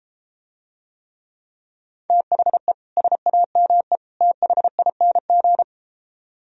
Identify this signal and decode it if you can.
Morse code, heard as a keyed tone, about 31 wpm, 700 Hz. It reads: THE SAME THING